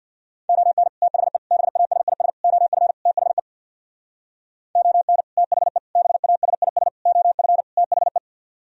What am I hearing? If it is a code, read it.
Morse code, heard as a keyed tone, about 64 wpm, 700 Hz. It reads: ON THE BASIS OF THE ON THE BASIS OF THE